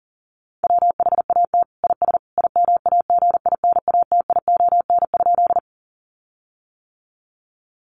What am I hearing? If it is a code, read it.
Morse code, heard as a keyed tone, about 40 wpm, 700 Hz. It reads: WHAT IS IMAGINATION?
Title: Sentences from Top 1300 Words 40wpm